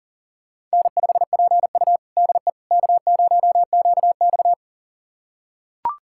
Morse code, heard as a keyed tone, about 40 wpm, 700 Hz. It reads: N5PU DE K0QX
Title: USA Callsign DE Callsign - 4 Characters - 40wpm